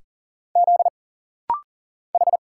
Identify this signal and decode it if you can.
Morse code, heard as a keyed tone, about 40 wpm, 700 Hz. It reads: Z H
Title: Single Letters - Rapid-Fire 40wpm